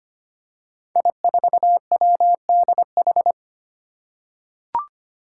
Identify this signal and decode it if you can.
Morse code, heard as a keyed tone, about 25 wpm, 700 Hz. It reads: I4WDH
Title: Five Challenging Letter-Number Combinations 25wpm